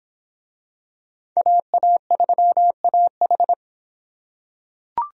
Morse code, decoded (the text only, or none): AA3AH